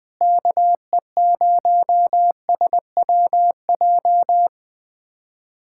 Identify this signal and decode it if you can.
Morse code, heard as a keyed tone, about 20 wpm, 700 Hz. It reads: KE0SWJ
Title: USA Callsign DE Callsign - 6 Characters - 20wpm